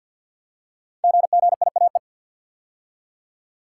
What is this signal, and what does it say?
Morse code, heard as a keyed tone, about 47 wpm, 700 Hz. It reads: GGIAE
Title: Code Groups - 5 Letters - 50wpm